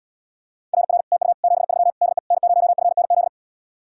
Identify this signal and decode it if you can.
Morse code, heard as a keyed tone, about 80 wpm, 700 Hz. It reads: RR TU 73 DE N0FNZ